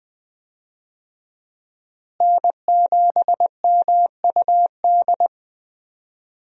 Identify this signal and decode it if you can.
Morse code, heard as a keyed tone, about 20 wpm, 700 Hz. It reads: N7MUD
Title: USA Call Signs - 5 Characters - Repeated 2x - 20wpm